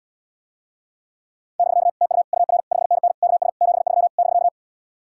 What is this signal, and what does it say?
Morse code, heard as a keyed tone, about 77 wpm, 700 Hz. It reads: <BK> TU UR 5NN KS 73 <BK>